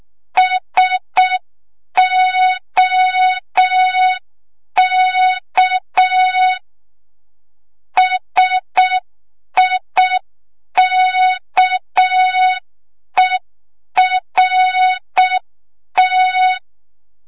Morse code, heard as a keyed tone, about 6 wpm, 3700 Hz. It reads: SOK SIKERT
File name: Morze-HA2MN / hang / 25 GYAKORLAT